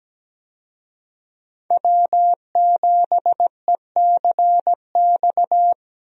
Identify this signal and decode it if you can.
Morse code, heard as a keyed tone, about 17 wpm, 700 Hz. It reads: W7ECX
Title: USA Call Signs - 5 Characters - Repeated 2x - 17wpm